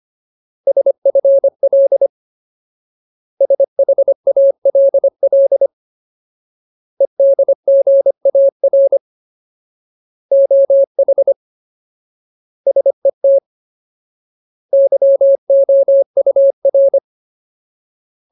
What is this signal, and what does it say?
Morse code, heard as a keyed tone, about 25 wpm, 550 Hz. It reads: SFL SHALL EDGAR OH SET YOUR
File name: Ultimate Word List / 25WPM Ultimate